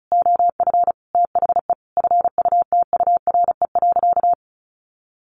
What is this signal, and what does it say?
Morse code, heard as a keyed tone, about 34 wpm, 700 Hz. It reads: OF THE FUTURE.
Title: Sentences from Top 1500 Words 35wpm